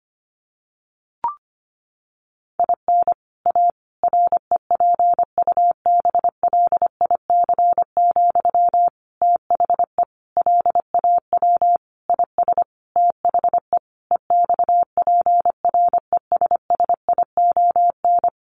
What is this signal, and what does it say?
Morse code, heard as a keyed tone, about 25 wpm, 700 Hz. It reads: IN A REPUBLIC, THE LAW IS THE EXPRESSION
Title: Sentences from Top 1500 Words 25wpm